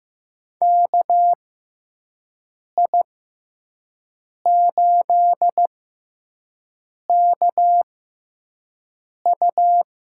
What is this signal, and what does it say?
Morse code, heard as a keyed tone, about 15 wpm, spaced as for 5 wpm, 700 Hz. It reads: KI8KU
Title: USA Call Signs - 5 Characters - 6x Character Spacing - 15wpm